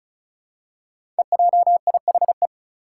Morse code, heard as a keyed tone, about 36 wpm, 700 Hz. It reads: EJIHE
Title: Five Challenging Letter-Number Combinations 35wpm